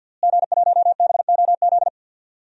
Morse code, heard as a keyed tone, about 48 wpm, 700 Hz. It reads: G1BOZ
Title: DXCC-100 Call Signs - 5 Characters - Repeated 3x - 50wpm